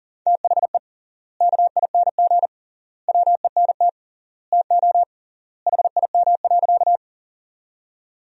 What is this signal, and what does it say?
Morse code, heard as a keyed tone, about 40 wpm, 700 Hz. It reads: THE KING WENT TO HIM.
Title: 3x Word-Spacing - Sentences from Top 400 Words - 40wpm